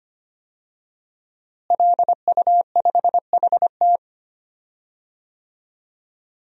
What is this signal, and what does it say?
Morse code, heard as a keyed tone, about 25 wpm, 700 Hz. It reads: LU5HT
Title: DXCC-100 Call Signs - 5 Characters - Repeated 2x - 25wpm